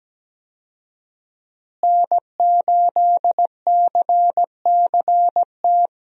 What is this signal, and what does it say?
Morse code, heard as a keyed tone, about 17 wpm, 700 Hz. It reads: N8CCT
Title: USA Call Signs - 5 Characters - Repeated 3x - 17wpm